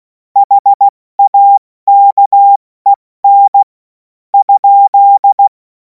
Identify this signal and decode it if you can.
Morse code, heard as a keyed tone, about 16 wpm, spaced as for 12 wpm, 800 Hz. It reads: HAKEN?
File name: Morsekurs / Witze / Witz 30. Juni 2026 70 F80